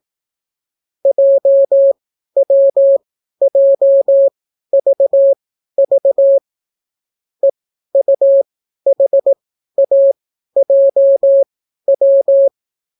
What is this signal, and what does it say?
Morse code, heard as a keyed tone, about 18 wpm, spaced as for 12 wpm, 550 Hz. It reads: JWJVV EUHAJW